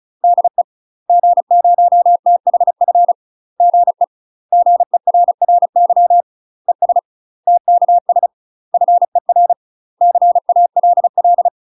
Morse code, heard as a keyed tone, about 33 wpm, 700 Hz. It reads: DE G0THF GE GERRY ES TKS FER CALL
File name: UK MASTER QSOs TRAINING/35WPM